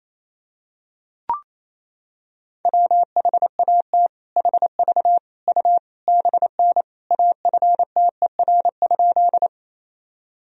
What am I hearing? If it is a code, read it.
Morse code, heard as a keyed tone, about 28 wpm, 700 Hz. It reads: WHAT HV U BN AFTER?
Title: Sentences from Top 200 Words and Abbreviations 28wpm